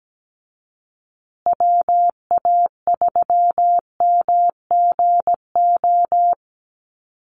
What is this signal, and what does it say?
Morse code, heard as a keyed tone, about 17 wpm, 700 Hz. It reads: WA3MGO
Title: USA Call Signs Repeated 3x 17wpm